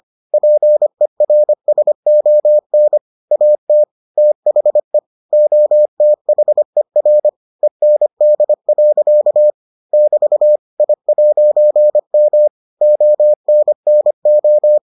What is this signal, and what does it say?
Morse code, heard as a keyed tone, about 25 wpm, 600 Hz. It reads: PERSON AT THE OTHER END. <BT> I'M ONNO